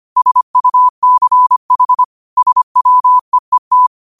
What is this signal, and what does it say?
Morse code, heard as a keyed tone, about 25 wpm, 1000 Hz. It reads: SUCH SWEET
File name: qotdfiles / 25wpm